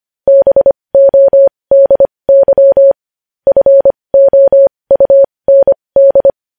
Morse code, heard as a keyed tone, about 25 wpm, spaced as for 20 wpm, 550 Hz. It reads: BODY FOUND
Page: News Headlines in Morse Code at 20 WPM